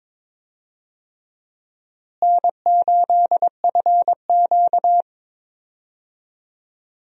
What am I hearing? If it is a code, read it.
Morse code, heard as a keyed tone, about 21 wpm, 700 Hz. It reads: N8FQ
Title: USA Call Signs - 4 Characters - Repeated 3x - 22wpm